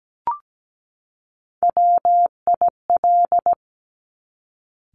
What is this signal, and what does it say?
Morse code, heard as a keyed tone, about 17 wpm, 690 Hz. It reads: WIL
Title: Top Three Letter Combinations in Words 17wpm